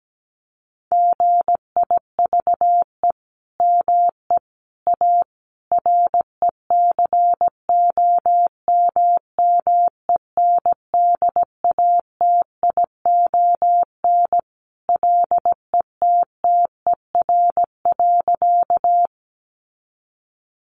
Morse code, heard as a keyed tone, about 17 wpm, 700 Hz. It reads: GIVE ME A RECOMMENDATION LETTER.
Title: Sentences from Top 1400 Words 17wpm